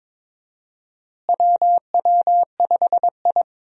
Morse code, heard as a keyed tone, about 22 wpm, 700 Hz. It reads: WW5I
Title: USA Call Signs - 4 Characters - Repeated 3x - 22wpm